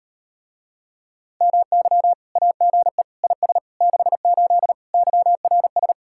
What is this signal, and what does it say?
Morse code, heard as a keyed tone, about 38 wpm, 700 Hz. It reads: MY AGE IS 68 YRS